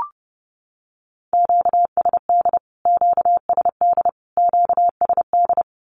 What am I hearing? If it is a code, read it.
Morse code, heard as a keyed tone, about 30 wpm, 700 Hz. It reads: QSD QSD QSD
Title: Q-Codes Repeated 3x 30wpm